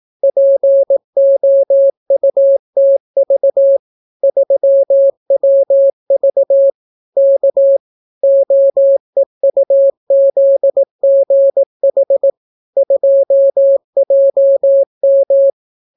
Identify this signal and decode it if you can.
Morse code, heard as a keyed tone, about 18 wpm, 550 Hz. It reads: POUTV 3WV K OEUZGH 2JM